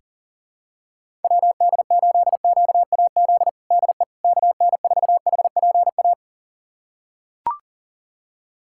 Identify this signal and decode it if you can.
Morse code, heard as a keyed tone, about 40 wpm, 700 Hz. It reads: WD8QAZ DE KN4HPA
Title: USA Callsign DE Callsign - 6 Characters - 40wpm